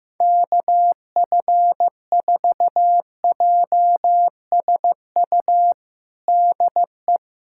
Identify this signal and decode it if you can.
Morse code, heard as a keyed tone, about 15 wpm, 700 Hz. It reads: KF4JSU DE
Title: USA Callsign DE Callsign 15wpm